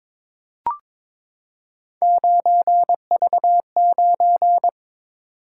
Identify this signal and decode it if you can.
Morse code, heard as a keyed tone, about 22 wpm, 700 Hz. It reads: 9V9
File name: practice / 22wpm / Three Letter-Number Combinations - 22wpm